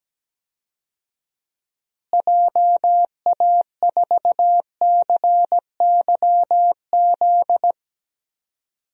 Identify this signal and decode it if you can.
Morse code, heard as a keyed tone, about 17 wpm, 700 Hz. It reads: JA4CYZ
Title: DXCC-100 Call Signs - 6 Characters - Repeated 3x - 17wpm